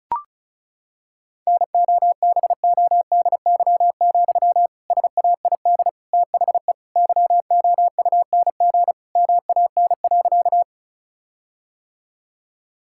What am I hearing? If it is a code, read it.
Morse code, heard as a keyed tone, about 34 wpm, 700 Hz. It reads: NOBODY, SAID THE YOUNG MAN.
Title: Sentences from Top 1600 Words 35wpm